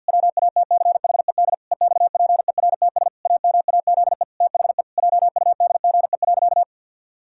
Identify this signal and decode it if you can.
Morse code, heard as a keyed tone, about 52 wpm, 700 Hz. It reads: WATCHED EXPERTS AMAZE THE JUDGE.
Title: Pangram Challenge v2 - 50wpm